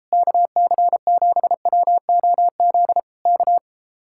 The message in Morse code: KC7WOZ K

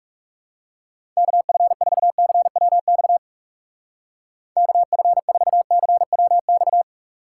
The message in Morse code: KF4CWX KF4CWX

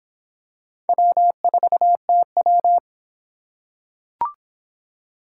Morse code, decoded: W4TW

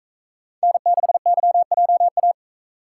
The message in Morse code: N6YJA

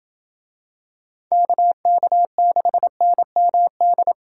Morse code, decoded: KK6NMD

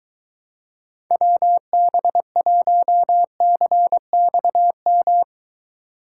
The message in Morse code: WB1CXM

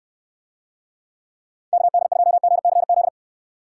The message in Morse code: KD2GCZ

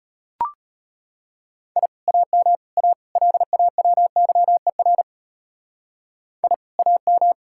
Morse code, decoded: I AM A LAWYER I AM

